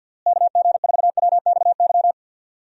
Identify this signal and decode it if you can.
Morse code, heard as a keyed tone, about 46 wpm, 700 Hz. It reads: KG4WXY
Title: USA Call Signs - 6 Characters - Repeated 3x - 50wpm